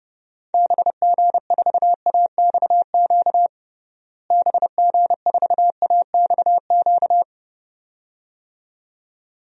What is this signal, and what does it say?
Morse code, heard as a keyed tone, about 32 wpm, 700 Hz. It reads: BG4AXQ BG4AXQ
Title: DXCC-100 Call Signs - 6 Characters - Repeated 2x - 30wpm